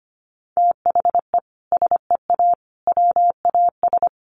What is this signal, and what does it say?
Morse code, heard as a keyed tone, about 25 wpm, 700 Hz. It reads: THE SEA WAS
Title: Sentences from Top 1200 Words 25wpm